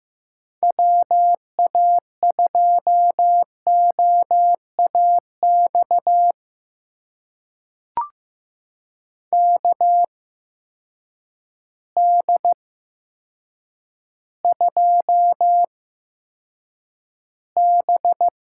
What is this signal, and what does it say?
Morse code, heard as a keyed tone, about 15 wpm, spaced as for 8 wpm, 700 Hz. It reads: WA2OAX KD2B
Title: USA Call Signs - 6 Characters - 8x Character Spacing - 15wpm